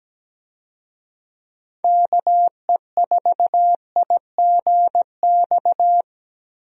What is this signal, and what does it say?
Morse code, heard as a keyed tone, about 17 wpm, 700 Hz. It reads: KE4IGX
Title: USA Call Signs - 6 Characters - Repeated 3x - 17wpm